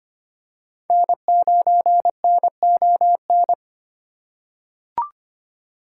Morse code, heard as a keyed tone, about 25 wpm, 700 Hz. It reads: N9NON
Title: USA Call Signs - 3x Character Spacing - 25wpm